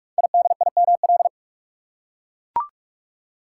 Morse code, heard as a keyed tone, about 44 wpm, 700 Hz. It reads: IDIML
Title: Code Groups - 5 Letter:Numbers - 45wpm